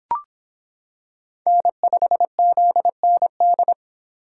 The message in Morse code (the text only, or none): N5ZND